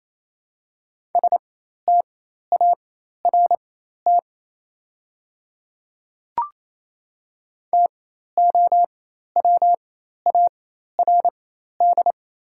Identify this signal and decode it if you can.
Morse code, heard as a keyed tone, about 28 wpm, spaced as for 13 wpm, 700 Hz. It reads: START TOWARD
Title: 4x Character Spacing - Top 500 Words - 28wpm